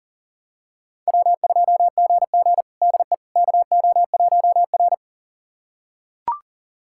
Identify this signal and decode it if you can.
Morse code, heard as a keyed tone, about 37 wpm, 700 Hz. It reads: W2GG DE KO1R
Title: USA Callsign DE Callsign - 4 Characters - 40wpm